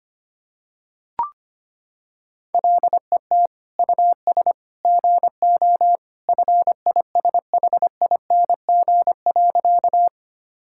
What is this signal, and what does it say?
Morse code, heard as a keyed tone, about 25 wpm, 700 Hz. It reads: LET US GO FISHING.
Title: Sentences from Top 1000 Words - Review Part 5 - 25wpm